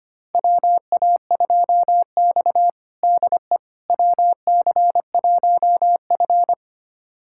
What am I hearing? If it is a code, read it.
Morse code, heard as a keyed tone, about 25 wpm, 700 Hz. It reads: WA2X DE WC1F
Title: USA Callsign DE Callsign - 4 Characters - 25wpm